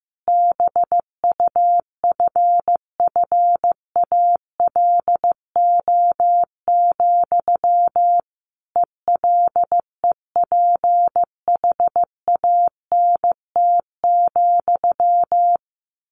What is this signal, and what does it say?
Morse code, heard as a keyed tone, about 15 wpm, 700 Hz. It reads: BUFFALO, ELEPHANT,
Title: A Sound Of Thunder 15wpm